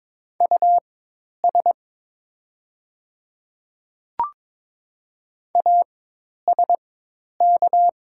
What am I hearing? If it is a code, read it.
Morse code, heard as a keyed tone, about 22 wpm, spaced as for 10 wpm, 700 Hz. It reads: US ASK
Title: 4x Character Spacing - Top 300 Words - 22wpm